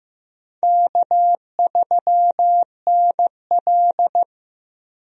Morse code, heard as a keyed tone, about 15 wpm, 700 Hz. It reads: K3NL
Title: USA Call Signs - 4 Characters - Repeated 3x - 15wpm